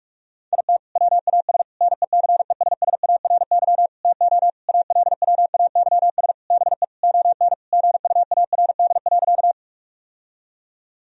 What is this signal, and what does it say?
Morse code, heard as a keyed tone, about 41 wpm, 700 Hz. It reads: IT WAS NECESSARY TO ALWAYS BE ON GUARD.
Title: Sentences from Top 1000 Words - Review Part 4 - 45wpm